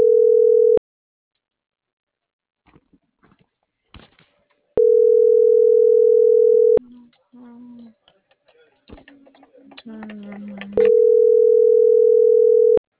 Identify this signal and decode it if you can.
Ringback tone (North America), 440 + 480 Hz.